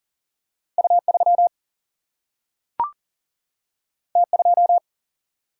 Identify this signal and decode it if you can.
Morse code, heard as a keyed tone, about 40 wpm, 700 Hz. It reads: U 3 T 2
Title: Single-Letter Single-Number 40wpm